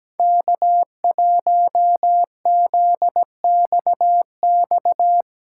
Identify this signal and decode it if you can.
Morse code, heard as a keyed tone, about 17 wpm, 700 Hz. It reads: K1ZXX
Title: USA Call Signs - 5 Characters - 4x Character Spacing - 17wpm